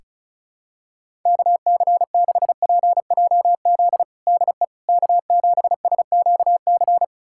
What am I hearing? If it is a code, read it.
Morse code, heard as a keyed tone, about 36 wpm, 700 Hz. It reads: KC6PJZ DE K7SQC
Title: USA Callsign DE Callsign 35wpm